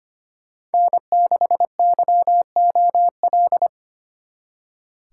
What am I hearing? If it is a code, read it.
Morse code, heard as a keyed tone, about 32 wpm, 700 Hz. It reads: N6YOL